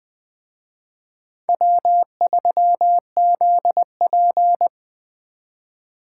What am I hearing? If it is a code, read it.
Morse code, heard as a keyed tone, about 20 wpm, 700 Hz. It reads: W3ZP
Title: Speed Racing - USA 1x2 Call Signs 17wpm